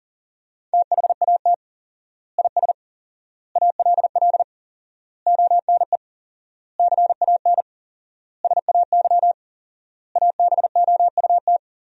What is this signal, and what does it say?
Morse code, heard as a keyed tone, about 40 wpm, 700 Hz. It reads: THAT IS ALL ONE CAN SAY ABOUT
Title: 4x Word-Spacing - Sentences from Top 200 Words - 40wpm